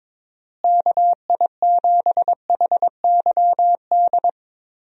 Morse code, heard as a keyed tone, about 21 wpm, 700 Hz. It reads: KI7HYD
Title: USA Call Signs - 6 Characters - Repeated 3x - 22wpm